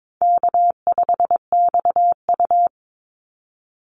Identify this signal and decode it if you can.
Morse code, heard as a keyed tone, about 22 wpm, 700 Hz. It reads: K5XU
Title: Easy Call Signs Repeated 3x - 22wpm